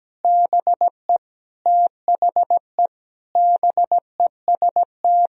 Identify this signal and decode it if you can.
Morse code, heard as a keyed tone, about 17 wpm, 700 Hz. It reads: BE THE BEST
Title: Sentences from Top 1000 Words - Review Part 2 - 17wpm